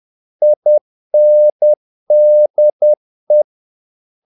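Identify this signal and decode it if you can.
Morse code, heard as a keyed tone, about 10 wpm, 600 Hz. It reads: INDE
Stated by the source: Morse practice of short Danish words at 10 wpm, from CW med Gnister.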